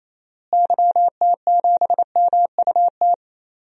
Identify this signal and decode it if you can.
Morse code, heard as a keyed tone, about 28 wpm, 700 Hz. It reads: YT7MUT
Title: DXCC-100 Call Signs - 6 Characters - Repeated 3x - 28wpm